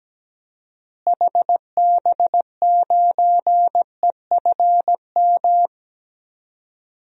Morse code, heard as a keyed tone, about 17 wpm, 700 Hz. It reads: HB9EFM
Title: DXCC-100 Call Signs - 6 Characters - Repeated 2x - 17wpm